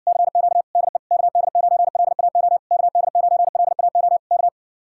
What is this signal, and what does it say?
Morse code, heard as a keyed tone, about 61 wpm, 700 Hz. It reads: CQ DE KD9LAO KD9LAO K